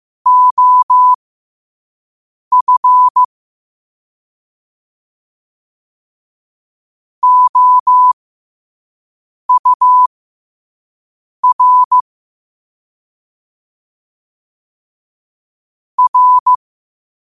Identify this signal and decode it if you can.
Morse code, heard as a keyed tone, about 15 wpm, spaced as for 5 wpm, 1000 Hz. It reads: OF OUR R
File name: qotdfiles / 5wpm